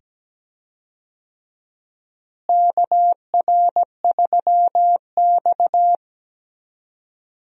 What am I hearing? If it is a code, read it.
Morse code, heard as a keyed tone, about 17 wpm, 700 Hz. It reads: KR3X
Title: USA Call Signs - 4 Characters - Repeated 2x - 17wpm